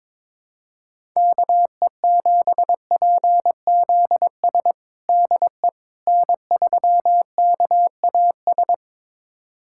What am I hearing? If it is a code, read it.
Morse code, heard as a keyed tone, about 22 wpm, 700 Hz. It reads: KE7PZS DE N3KAS
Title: USA Callsign DE Callsign 22wpm